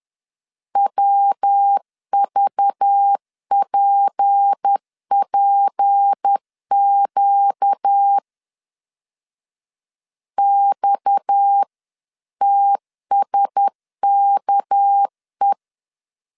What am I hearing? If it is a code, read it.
Morse code, heard as a keyed tone, about 10 wpm, 800 Hz. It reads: WVPPQ XTSKE
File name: CW / DET / 1 6 1500C S B 10wpm 4137